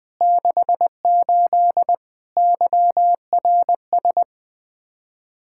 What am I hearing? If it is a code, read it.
Morse code, heard as a keyed tone, about 20 wpm, 700 Hz. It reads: 68 YRS
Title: Speed-Racing Common QSO Phrases - 17wpm